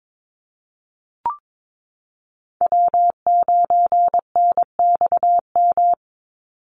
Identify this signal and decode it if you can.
Morse code, heard as a keyed tone, about 22 wpm, 700 Hz. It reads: W9NXM
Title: Call Signs Repeated 3x - 22wpm